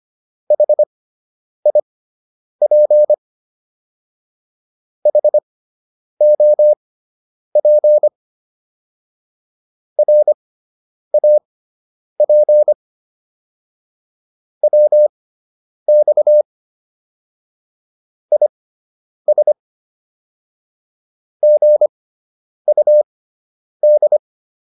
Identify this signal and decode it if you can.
Morse code, heard as a keyed tone, about 25 wpm, spaced as for 9 wpm, 600 Hz. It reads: HIP HOP RAP WX IS GUD